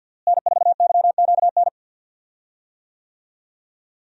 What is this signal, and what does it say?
Morse code, heard as a keyed tone, about 53 wpm, 700 Hz. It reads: N4YQN
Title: USA Call Signs - 5 Characters - Repeated 3x - 50wpm